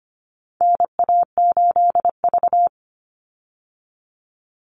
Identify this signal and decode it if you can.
Morse code, heard as a keyed tone, about 25 wpm, 700 Hz. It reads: NA8V
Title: Call Signs - 25wpm